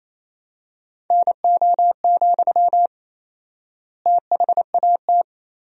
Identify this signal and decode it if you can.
Morse code, heard as a keyed tone, about 28 wpm, 700 Hz. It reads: NO, THAT
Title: 4x Word-Spacing - Sentences from Top 300 Words - 28wpm